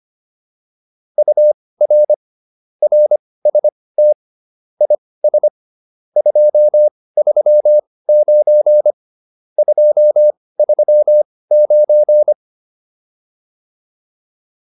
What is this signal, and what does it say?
Morse code, heard as a keyed tone, about 25 wpm, spaced as for 18 wpm, 600 Hz. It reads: UR RST IS 239 239
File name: QSO308 18